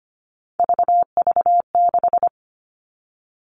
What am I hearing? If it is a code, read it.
Morse code, heard as a keyed tone, about 25 wpm, 700 Hz. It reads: VV6
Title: All Cut Numbers - 3-Digit 25wpm